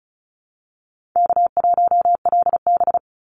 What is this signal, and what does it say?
Morse code, heard as a keyed tone, about 34 wpm, 700 Hz. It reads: K1LB
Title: NA SOTA Chaser Callsigns 35wpm